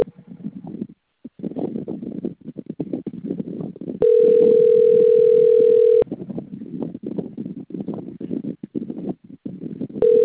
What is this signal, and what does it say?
Ringback tone (North America), 440 + 480 Hz.
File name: out- - - - - .9573